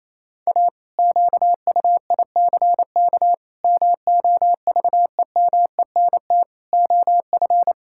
Morse code, heard as a keyed tone, about 27 wpm, 700 Hz. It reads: A QUICK MOVEMENT OF
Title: Pangram Challenge v2 - 28wpm